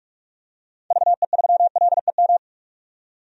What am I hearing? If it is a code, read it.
Morse code, heard as a keyed tone, about 46 wpm, 700 Hz. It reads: VE3LEM
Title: DXCC-100 Call Signs - 6 Characters - 45wpm